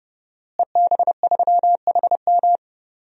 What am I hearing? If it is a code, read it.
Morse code, heard as a keyed tone, about 30 wpm, 700 Hz. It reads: EB3HM
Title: DXCC-100 Call Signs - 5 Characters - Repeated 2x - 30wpm